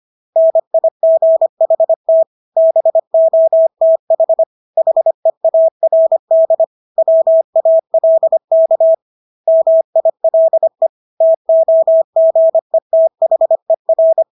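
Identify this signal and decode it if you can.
Morse code, heard as a keyed tone, about 25 wpm, 650 Hz. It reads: NIGHT BOTH HEARD WALK MILE TOGETHER
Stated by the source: Morse practice of English words at 25 wpm from CW med Gnister.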